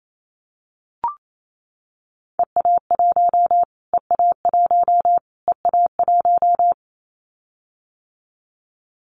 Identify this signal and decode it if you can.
Morse code, heard as a keyed tone, about 28 wpm, 700 Hz. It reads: EA1 EA1 EA1
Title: Callsign Prefixes for Top 25 DXCC - Repeated 3x - 28wpm